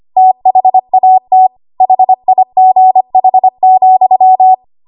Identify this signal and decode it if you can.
Morse code, heard as a keyed tone, about 25 wpm, 750 Hz. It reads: THAT HIGH,
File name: Archive / 25 WPM / 25WPM